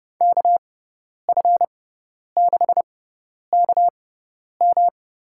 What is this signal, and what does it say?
Morse code, heard as a keyed tone, about 30 wpm, spaced as for 10 wpm, 700 Hz. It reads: KF6KM
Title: USA Call Signs - 6 Characters - 6x Character Spacing - 30wpm